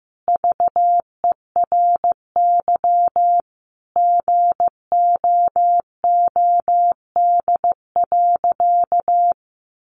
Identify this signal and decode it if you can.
Morse code, heard as a keyed tone, about 14 wpm, 700 Hz. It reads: VERY GOOD.
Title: Sentences from Top 800 Words 15wpm